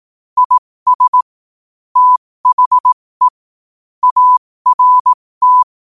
Morse code, heard as a keyed tone, about 18 wpm, spaced as for 15 wpm, 1000 Hz. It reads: IS THE ART